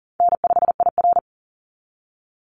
Morse code, heard as a keyed tone, about 40 wpm, 700 Hz. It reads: N5IR
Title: Easy Call Signs - 40wpm